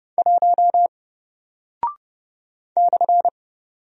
Morse code, heard as a keyed tone, about 30 wpm, 700 Hz. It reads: 1 /
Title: Single Character (Letter:Number:Punctuation:Prosign) Alt 30wpm